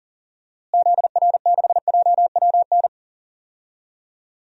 Morse code, heard as a keyed tone, about 37 wpm, 700 Hz. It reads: ZR6JWN
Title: DXCC-100 Call Signs - 6 Characters - Repeated 2x - 40wpm